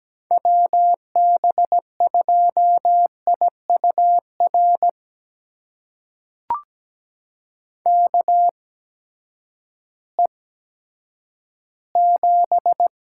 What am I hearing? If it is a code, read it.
Morse code, heard as a keyed tone, about 17 wpm, spaced as for 10 wpm, 700 Hz. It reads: WB2IUR KE7
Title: USA Call Signs - 6 Characters - 8x Character Spacing - 17wpm